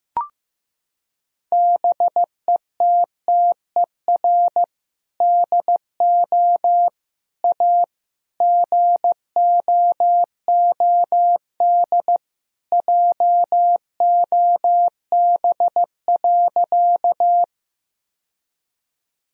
Morse code, heard as a keyed tone, about 15 wpm, 700 Hz. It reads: BETTER DO A GOOD JOB.
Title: Sentences from Top 1000 Words - Review Part 3 - 15wpm